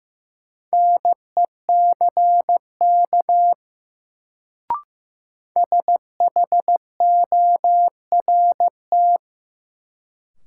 Morse code, heard as a keyed tone, about 15 wpm, 700 Hz. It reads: NECK SHORT
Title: Sets of 5 Words Encore - New Word Drill - 15wpm